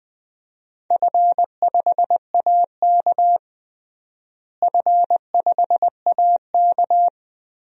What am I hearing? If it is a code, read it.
Morse code, heard as a keyed tone, about 20 wpm, 700 Hz. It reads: F5AK F5AK
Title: DXCC-100 Call Signs - 4 Characters - Repeated 2x - 20wpm